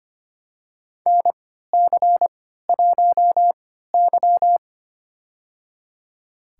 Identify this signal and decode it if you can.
Morse code, heard as a keyed tone, about 25 wpm, spaced as for 14 wpm, 700 Hz. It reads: NC1Y
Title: USA Call Signs - 4 Characters - 3x Character Spacing - 25wpm